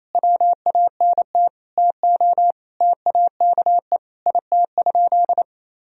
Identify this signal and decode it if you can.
Morse code, heard as a keyed tone, about 28 wpm, 700 Hz. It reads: WANT TO TAKE IT?
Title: Sentences from Top 300 Words and Abbreviations 28wpm